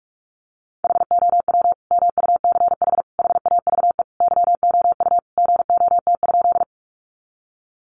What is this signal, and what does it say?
Morse code, heard as a keyed tone, about 44 wpm, 700 Hz. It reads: HOW MUCH HAVE YOU GOT?
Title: Sentences from Top 300 Words 45wpm